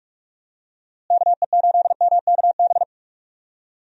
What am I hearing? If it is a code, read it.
Morse code, heard as a keyed tone, about 46 wpm, 700 Hz. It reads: KE8MKB
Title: USA Call Signs - 6 Characters - Repeated 2x - 45wpm